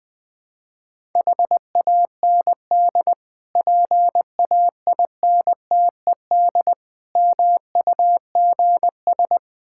Morse code, heard as a keyed tone, about 20 wpm, 700 Hz. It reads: HAND PAINTED MUGS
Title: Difficult Words with Hints - 20wpm